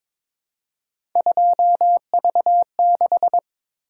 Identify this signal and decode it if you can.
Morse code, heard as a keyed tone, about 22 wpm, 700 Hz. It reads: 2V6
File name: practice / 22wpm / Three Letter-Number Combinations - 22wpm